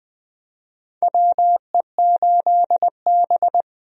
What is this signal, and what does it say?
Morse code, heard as a keyed tone, about 20 wpm, 700 Hz. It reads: WE8B
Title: USA Call Signs - 4 Characters - Repeated 2x - 20wpm